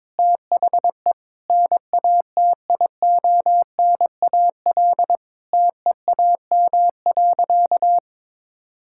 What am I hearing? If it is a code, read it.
Morse code, heard as a keyed tone, about 22 wpm, 700 Hz. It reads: THE NATIONAL TEAM.
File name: practice / 22wpm / Sentences with 3-Word Phrases - 22wpm